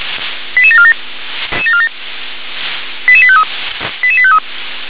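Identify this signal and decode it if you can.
Five-tone selective call (ZVEI), heard as quick runs of five tones, tones of about 70 ms. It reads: 88747 88853 88853
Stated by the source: Received on SDR as ZVEI five-tone selective call 88702.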